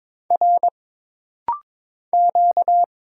Morse code, heard as a keyed tone, about 22 wpm, 700 Hz. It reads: R Q
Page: This rapid-fire timing is geared to push your ICR (Instant Character Recognition) skills.